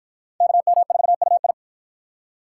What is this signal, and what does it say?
Morse code, heard as a keyed tone, about 50 wpm, 700 Hz. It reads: KM4UI